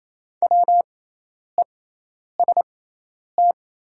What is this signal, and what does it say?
Morse code, heard as a keyed tone, about 28 wpm, spaced as for 10 wpm, 700 Hz. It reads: WEST
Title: 6x Character Spacing - Top 500 Words - 28wpm